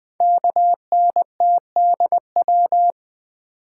Morse code, heard as a keyed tone, about 20 wpm, 700 Hz. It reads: KNTDW
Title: Code Groups - 5 Letters - 20wpm